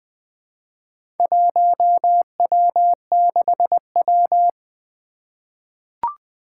Morse code, heard as a keyed tone, about 20 wpm, 700 Hz. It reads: 1W6W
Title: Four Challenging Letter-Number Combinations 20wpm